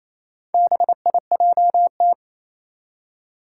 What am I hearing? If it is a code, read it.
Morse code, heard as a keyed tone, about 27 wpm, 700 Hz. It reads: BIJT
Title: Four Challenging Letter-Number Combinations 28wpm